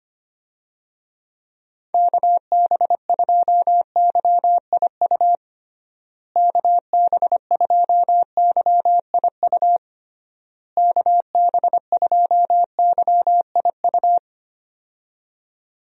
Morse code, heard as a keyed tone, about 25 wpm, 700 Hz. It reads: KB2YIU KB2YIU KB2YIU
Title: USA Call Signs - 6 Characters - Repeated 3x - 25wpm